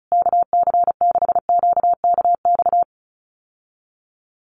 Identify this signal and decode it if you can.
Morse code, heard as a keyed tone, about 34 wpm, 700 Hz. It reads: KC6QKX
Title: USA Call Signs Repeated 3x 35wpm